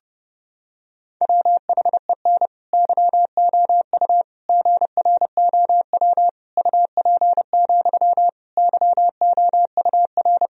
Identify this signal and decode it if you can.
Morse code, heard as a keyed tone, about 32 wpm, 700 Hz. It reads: WHEN YOU GROW UP, YOUR
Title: Sentences from Top 1000 Words - Review Part 1 - 30wpm